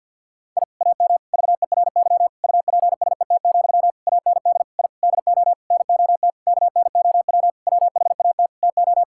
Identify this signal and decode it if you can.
Morse code, heard as a keyed tone, about 46 wpm, 700 Hz. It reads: I AM VERY UPSET, AND I DO NOT KNOW WHAT TO
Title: Sentences from Top 1000 Words - Review Part 5 - 50wpm